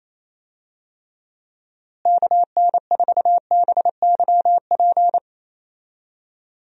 Morse code, heard as a keyed tone, about 28 wpm, 700 Hz. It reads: KN4BYP
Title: USA Call Signs - 6 Characters - Repeated 2x - 28wpm